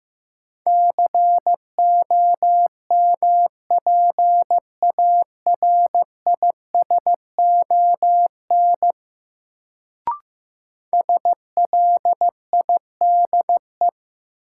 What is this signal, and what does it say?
Morse code, heard as a keyed tone, about 15 wpm, 700 Hz. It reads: COMPARISON SLIDE
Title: Sentences from Top 1600 Words - New Word Drill - 15wpm